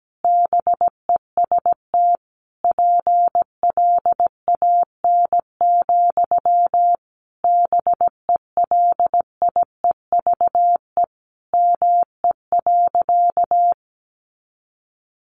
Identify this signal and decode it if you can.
Morse code, heard as a keyed tone, about 17 wpm, 700 Hz. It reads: BEST PLAN, BELIEVE ME.
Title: Sentences from Top 1000 Words 17wpm